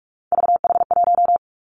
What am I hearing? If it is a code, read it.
Morse code, heard as a keyed tone, about 46 wpm, 700 Hz. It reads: VH1
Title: Three Challenging Letter-Number Combinations 45wpm